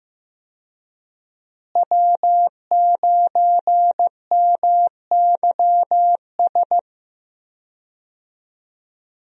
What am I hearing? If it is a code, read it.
Morse code, heard as a keyed tone, about 15 wpm, 700 Hz. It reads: W9MYS
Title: USA Call Signs - 5 Characters - Repeated 2x - 15wpm